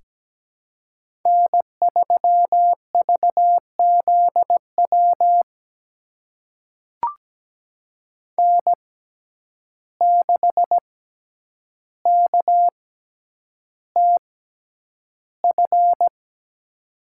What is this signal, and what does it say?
Morse code, heard as a keyed tone, about 17 wpm, spaced as for 9 wpm, 700 Hz. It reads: N3VZW N6KTF
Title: USA Call Signs - 5 Characters - 6x Character Spacing - 17wpm